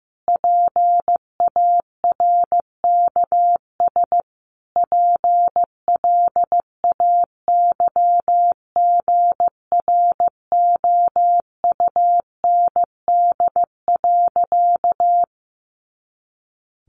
Morse code, heard as a keyed tone, about 15 wpm, 700 Hz. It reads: PARKS PLAYGROUND.
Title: Additional Easy Phrases 15wpm